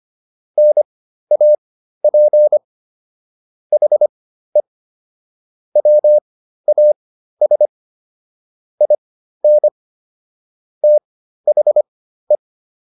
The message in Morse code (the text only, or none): NAP HE WAS IN THE